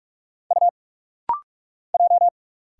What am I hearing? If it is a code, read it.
Morse code, heard as a keyed tone, about 44 wpm, 700 Hz. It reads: U J
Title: Single Letters - Rapid-Fire 45wpm